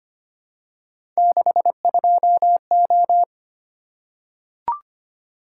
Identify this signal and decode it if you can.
Morse code, heard as a keyed tone, about 25 wpm, 700 Hz. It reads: 62O
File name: Three Letter-Number Combinations - 25wpm